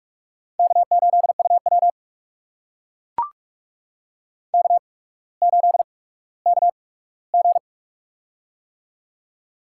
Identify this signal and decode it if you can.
Morse code, heard as a keyed tone, about 44 wpm, spaced as for 19 wpm, 700 Hz. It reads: K8UW K8KG
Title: USA Call Signs - 4 Characters - 8x Character Spacing - 45wpm